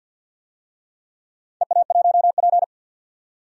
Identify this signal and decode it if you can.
Morse code, heard as a keyed tone, about 47 wpm, 700 Hz. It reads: EA1P
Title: DXCC-100 Call Signs - 4 Characters - 50wpm